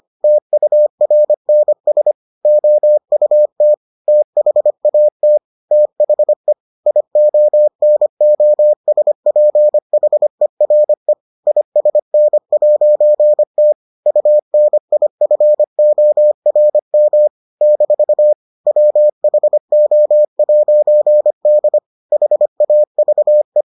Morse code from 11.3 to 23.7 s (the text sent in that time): ISN'T UNIFORM - WHO'D HAVE